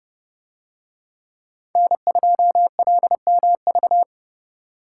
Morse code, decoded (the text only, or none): N2LMV